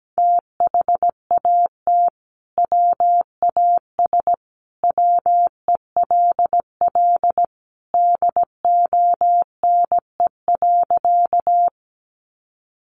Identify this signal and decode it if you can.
Morse code, heard as a keyed tone, about 17 wpm, 700 Hz. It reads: THAT WAS WELL DONE.
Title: Sentences from Top 500 Words 17wpm